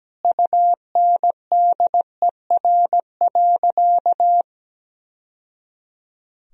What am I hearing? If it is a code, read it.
Morse code, heard as a keyed tone, about 17 wpm, 700 Hz. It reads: UNDER.
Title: Sentences from Top 1000 Words - Review Part 1 - 17wpm